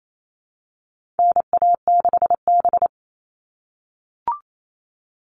Morse code, heard as a keyed tone, about 28 wpm, 700 Hz. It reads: NA6B
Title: All Cut Numbers - 4-Digit 28wpm